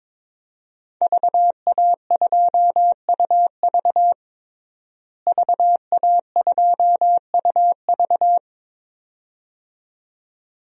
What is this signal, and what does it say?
Morse code, heard as a keyed tone, about 22 wpm, 700 Hz. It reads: VA2UV VA2UV
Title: DXCC-100 Call Signs - 5 Characters - Repeated 2x - 22wpm